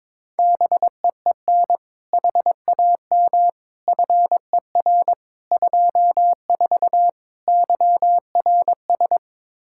Morse code, heard as a keyed tone, about 22 wpm, 700 Hz. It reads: BEEN HAM FER 24 YRS
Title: Practice QSO - Exchange 3 - More Info - 22wpm